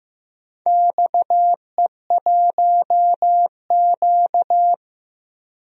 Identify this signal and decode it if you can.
Morse code, heard as a keyed tone, about 15 wpm, 700 Hz. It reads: XE1Q
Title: DXCC-100 Call Signs - 4 Characters - Repeated 2x - 15wpm